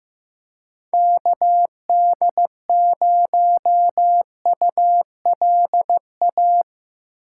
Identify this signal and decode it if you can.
Morse code, heard as a keyed tone, about 15 wpm, 700 Hz. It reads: KD0ULA
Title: USA Call Signs - 6 Characters - 4x Character Spacing - 15wpm